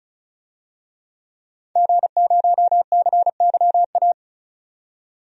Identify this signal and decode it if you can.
Morse code, heard as a keyed tone, about 36 wpm, 700 Hz. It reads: G0CYA
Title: DXCC-100 Call Signs - 5 Characters - 35wpm